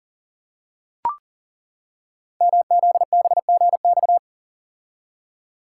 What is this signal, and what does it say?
Morse code, heard as a keyed tone, about 42 wpm, 690 Hz. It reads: MZBGX
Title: Code Groups - 5 Letters - 40wpm